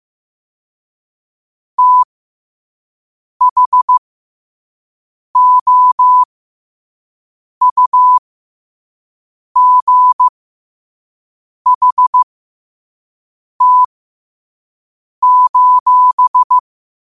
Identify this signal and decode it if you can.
Morse code, heard as a keyed tone, about 15 wpm, spaced as for 5 wpm, 1000 Hz. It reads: THOUGHT:
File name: 5wpm